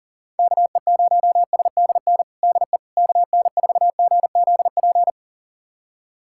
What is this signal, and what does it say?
Morse code, heard as a keyed tone, about 40 wpm, 700 Hz. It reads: KE0SDN DE KN4GZP
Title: USA Callsign DE Callsign - 6 Characters - 40wpm